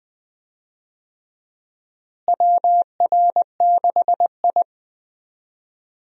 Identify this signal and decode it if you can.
Morse code, heard as a keyed tone, about 20 wpm, 700 Hz. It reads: WR6I
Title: USA Call Signs - 4 Characters - Repeated 2x - 20wpm